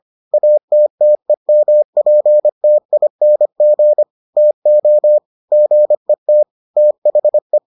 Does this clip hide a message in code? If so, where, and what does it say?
Yes: Morse code from 0.3 to 7.6 s, heard as a keyed tone, about 25 wpm, 600 Hz. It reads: ATTEMPTING TO GET THE